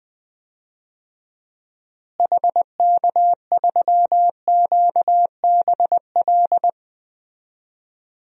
Morse code, heard as a keyed tone, about 20 wpm, 700 Hz. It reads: HK3QBL
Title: DXCC-100 Call Signs - 6 Characters - Repeated 2x - 20wpm